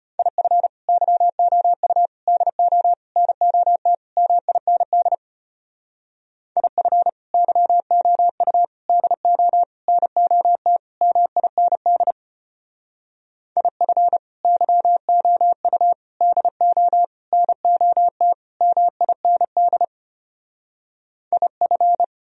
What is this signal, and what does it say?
Morse code, heard as a keyed tone, about 32 wpm, 700 Hz. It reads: IF YOU DO NOT MIND IF YOU DO NOT MIND IF YOU DO NOT MIND IF